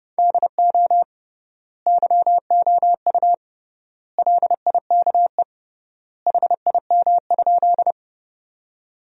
Morse code, heard as a keyed tone, about 30 wpm, 700 Hz. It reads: DO YOU LIKE HIM?
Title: 3x Word-Spacing - Sentences from Top 100 Words - 30wpm